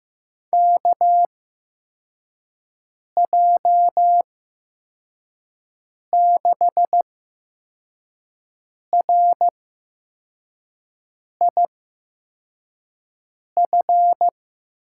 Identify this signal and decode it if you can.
Morse code, heard as a keyed tone, about 15 wpm, spaced as for 4 wpm, 700 Hz. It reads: KJ6RIF
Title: USA Call Signs - 8x Character Spacing - 15wpm